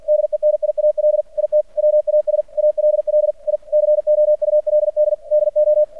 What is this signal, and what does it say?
Morse code, heard as a keyed tone, about 45 wpm, 600 Hz. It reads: DENIAL IT WAS ALL I COULD DO
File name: archived / CRSnet